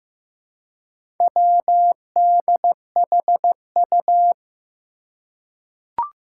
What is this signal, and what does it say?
Morse code, heard as a keyed tone, about 15 wpm, 700 Hz. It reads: WDHU
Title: Four Challenging Letter-Number Combinations 15wpm